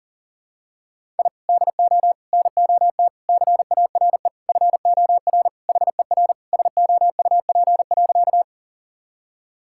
Morse code, heard as a keyed tone, about 40 wpm, 700 Hz. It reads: I DO NOT CARE FOR HER SOUP.
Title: Sentences from Top 1000 Words - Review Part 6 - 40wpm